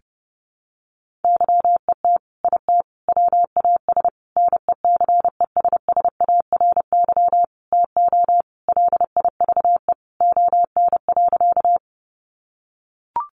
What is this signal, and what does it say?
Morse code, heard as a keyed tone, about 30 wpm, 700 Hz. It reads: YET IT WAS NECESSARY TO LIVE ON.
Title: Sentences from Top 800 Words 30wpm